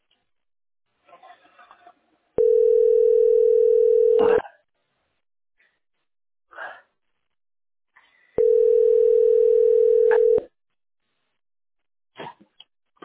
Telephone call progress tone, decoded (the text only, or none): ringback tone